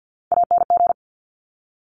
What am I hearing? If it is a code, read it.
Morse code, heard as a keyed tone, about 50 wpm, 700 Hz. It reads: ANG
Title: Top Three Letter Combinations in Words 50wpm